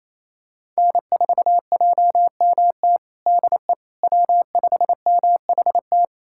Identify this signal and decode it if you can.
Morse code, heard as a keyed tone, about 28 wpm, 700 Hz. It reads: N4JMT DE W5MHT
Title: USA Callsign DE Callsign - 5 Characters - 28wpm